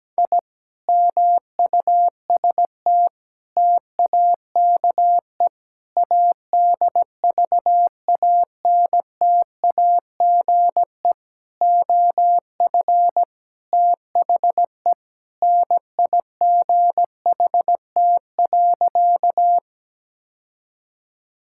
Morse code, heard as a keyed tone, about 17 wpm, 700 Hz. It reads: I MUST TAKE ADVANTAGE OF THE NIGHT.
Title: Sentences from Top 1000 Words - Review Part 6 - 17wpm